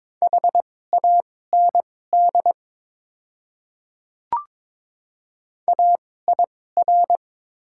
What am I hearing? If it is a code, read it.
Morse code, heard as a keyed tone, about 22 wpm, spaced as for 16 wpm, 700 Hz. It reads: HAND AIR
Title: 2x Character Spacing - Top 200 Words - 22wpm